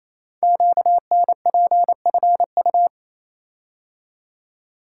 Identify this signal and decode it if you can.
Morse code, heard as a keyed tone, about 28 wpm, 700 Hz. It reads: QNPFU
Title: Code Groups - 5 Letters - 28wpm